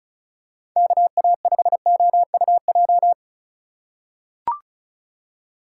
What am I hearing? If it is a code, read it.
Morse code, heard as a keyed tone, about 36 wpm, 700 Hz. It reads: KA5OUJ
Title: USA Call Signs - 6 Characters - 2x Character Spacing - 35wpm